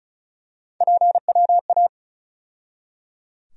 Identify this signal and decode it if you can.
Morse code, heard as a keyed tone, about 36 wpm, 700 Hz. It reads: PWA
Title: USA Call Sign Postfixes - 3 Letters 35wpm